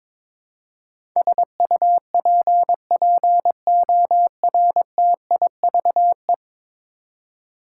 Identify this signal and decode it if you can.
Morse code, heard as a keyed tone, about 22 wpm, 700 Hz. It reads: SUPPORTIVE
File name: practice / 22wpm / 10 Letter Words - First 3 Letters Given - 22wpm